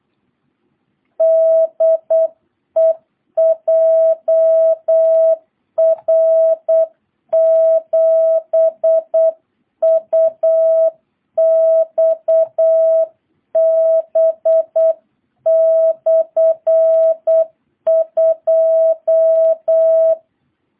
Morse code, heard as a keyed tone, about 8 wpm, 660 Hz. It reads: DE JR7UXB/2